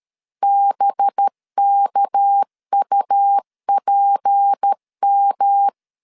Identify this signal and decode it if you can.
Morse code, heard as a keyed tone, about 13 wpm, 800 Hz. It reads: BKUPM
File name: LEZIONI / 6 PKX0 / 6 PK 12wpm 2006